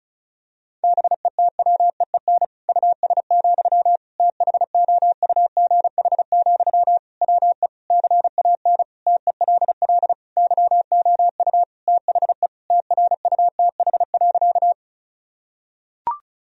Morse code, heard as a keyed tone, about 34 wpm, 700 Hz. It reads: BETWEEN US, THOUGH, WE CAN TELL YOU THE TRUTH.
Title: Sentences from Top 1000 Words - Review Part 2 - 35wpm